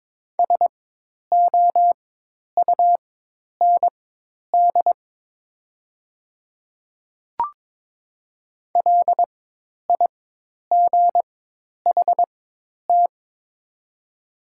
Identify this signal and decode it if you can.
Morse code, heard as a keyed tone, about 22 wpm, spaced as for 10 wpm, 700 Hz. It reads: SOUND LIGHT
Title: 4x Character Spacing - Top 200 Words - 22wpm